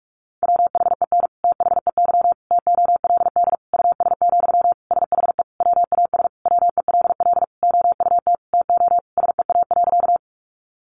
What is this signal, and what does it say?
Morse code, heard as a keyed tone, about 44 wpm, 700 Hz. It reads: WHEN THEY TOLD US, SHE WAS WELL OUT TO SEA.
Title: Sentences from Top 400 Words 45wpm